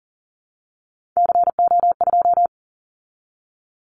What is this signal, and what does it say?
Morse code, heard as a keyed tone, about 40 wpm, 700 Hz. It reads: CO2
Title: Callsign Prefixes for Top 100 DXCC - 40wpm